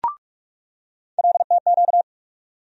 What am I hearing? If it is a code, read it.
Morse code, heard as a keyed tone, about 45 wpm, 700 Hz. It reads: LTQ
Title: USA Call Sign Postfixes - 3 Letters 45wpm